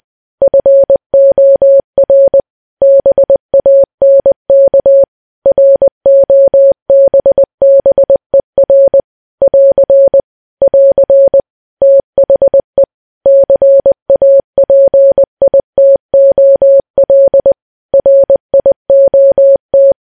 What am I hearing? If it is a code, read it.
Morse code, heard as a keyed tone, about 20 wpm, 550 Hz. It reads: FOR BANK ROBBER <AR> <AR> THE CAPITOL RIOT
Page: news-headlines-in-morse-code-at-20-wpm